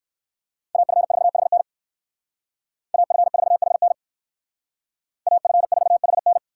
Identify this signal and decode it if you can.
Morse code, heard as a keyed tone, about 58 wpm, 700 Hz. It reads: AF4HN AF4HN AF4HN